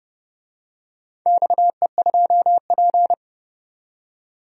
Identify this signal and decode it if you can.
Morse code, heard as a keyed tone, about 32 wpm, 700 Hz. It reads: XE2P
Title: DXCC-100 Call Signs - 4 Characters - 30wpm